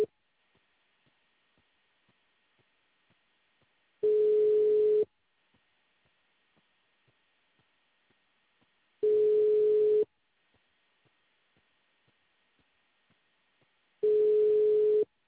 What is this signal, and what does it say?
Ringback tone (Europe, 425 Hz).